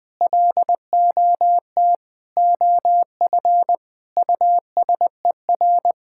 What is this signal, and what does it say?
Morse code, heard as a keyed tone, about 20 wpm, 700 Hz. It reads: LOT OF USER
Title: Sentences from Top 1000 Words - Review Part 6 - 20wpm